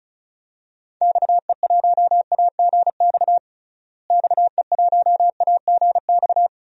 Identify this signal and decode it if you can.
Morse code, heard as a keyed tone, about 36 wpm, 700 Hz. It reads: XE1AGX XE1AGX
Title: DXCC-100 Call Signs - 6 Characters - Repeated 2x - 35wpm